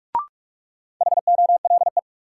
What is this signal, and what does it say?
Morse code, heard as a keyed tone, about 41 wpm, 700 Hz. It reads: HOLE
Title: Sentences from Top 1600 Words - New Word Drill Alt - 45wpm